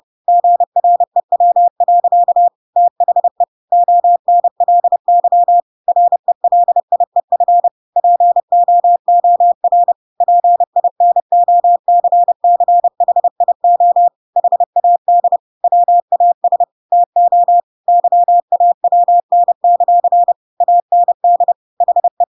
Morse code, heard as a keyed tone, about 30 wpm, 700 Hz. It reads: GREW. THE ONLY RELIEF POOR PINOCCHIO HAD WAS TO YAWN; AND HE
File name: Pinocchio / Pinocchio ch5 30 30 ik1hge